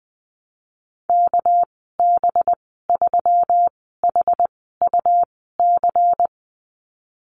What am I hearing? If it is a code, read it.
Morse code, heard as a keyed tone, about 20 wpm, spaced as for 14 wpm, 700 Hz. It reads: KB3HUC
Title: 207 Lesson 33g - Letter j - Call Signs 20wpm